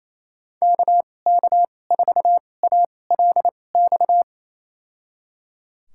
Morse code, heard as a keyed tone, about 28 wpm, spaced as for 20 wpm, 700 Hz. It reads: KK4ALX
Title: USA Call Signs - 6 Characters - 2x Character Spacing - 28wpm